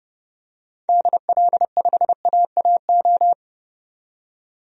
Morse code, heard as a keyed tone, about 31 wpm, 700 Hz. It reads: DL5AAO
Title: DXCC-100 Call Signs - 6 Characters - 30wpm